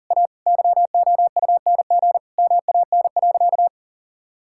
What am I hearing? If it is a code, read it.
Morse code, heard as a keyed tone, about 40 wpm, 700 Hz. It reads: A YOUNG MAN.
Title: Sentences from Top 1000 Words - Review Part 5 - 40wpm